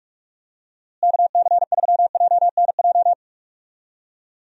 Morse code, heard as a keyed tone, about 46 wpm, 700 Hz. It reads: KC3JNJ
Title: USA Call Signs - 6 Characters - Repeated 2x - 45wpm